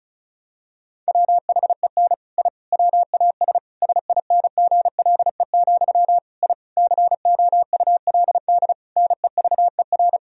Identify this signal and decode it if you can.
Morse code, heard as a keyed tone, about 34 wpm, 700 Hz. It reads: WHEN I WAS SINGLE, I COULD NEVER
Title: Sentences from Top 1000 Words - Review Part 4 - 35wpm